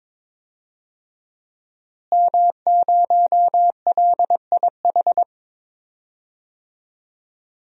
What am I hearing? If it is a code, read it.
Morse code, heard as a keyed tone, about 22 wpm, 700 Hz. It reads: M0LIH
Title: DXCC-100 Call Signs - 5 Characters - Repeated 2x - 22wpm